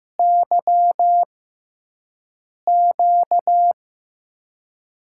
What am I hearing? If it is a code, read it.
Morse code, heard as a keyed tone, about 15 wpm, 700 Hz. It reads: Y Q
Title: USA Call Signs - 6 Characters - 6x Character Spacing - 15wpm